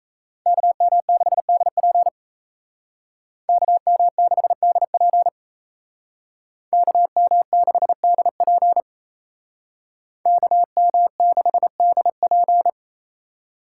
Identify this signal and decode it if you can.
Morse code, heard as a keyed tone, about 35 wpm, 700 Hz. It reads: KM6DP KM6DP KM6DP KM6DP